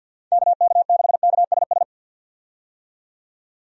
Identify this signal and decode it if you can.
Morse code, heard as a keyed tone, about 48 wpm, 700 Hz. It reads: KK6KSS
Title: USA Call Signs - 6 Characters - 50wpm